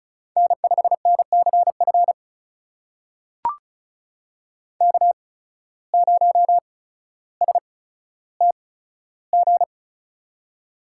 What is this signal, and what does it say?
Morse code, heard as a keyed tone, about 34 wpm, spaced as for 15 wpm, 700 Hz. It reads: N5NCF K0STG
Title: USA Call Signs - 8x Character Spacing - 35wpm